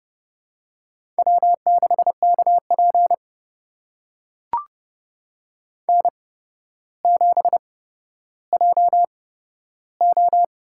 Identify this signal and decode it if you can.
Morse code, heard as a keyed tone, about 32 wpm, spaced as for 13 wpm, 700 Hz. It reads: W6KP N7JO
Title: USA Call Signs - 4 Characters - 8x Character Spacing - 30wpm